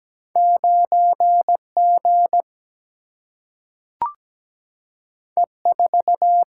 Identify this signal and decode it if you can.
Morse code, heard as a keyed tone, about 17 wpm, 700 Hz. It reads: 9G E4
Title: Single-Letter Single-Number 17wpm